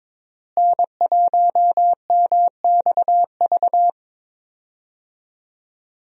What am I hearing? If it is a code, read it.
Morse code, heard as a keyed tone, about 22 wpm, 700 Hz. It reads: N1MXV
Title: USA Call Signs - 5 Characters - Repeated 3x - 22wpm